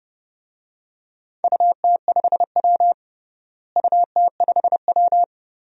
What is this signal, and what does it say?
Morse code, heard as a keyed tone, about 30 wpm, 700 Hz. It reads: UT5W UT5W
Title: DXCC-100 Call Signs - 4 Characters - Repeated 2x - 30wpm